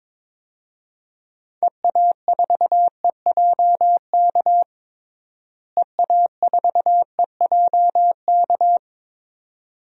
Morse code, heard as a keyed tone, about 22 wpm, 700 Hz. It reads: EA4EJK EA4EJK
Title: DXCC-100 Call Signs - 6 Characters - Repeated 2x - 22wpm